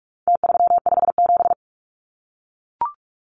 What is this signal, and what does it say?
Morse code, heard as a keyed tone, about 44 wpm, 700 Hz. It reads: T357
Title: Numbers - Contest Style - 4-Digit 45wpm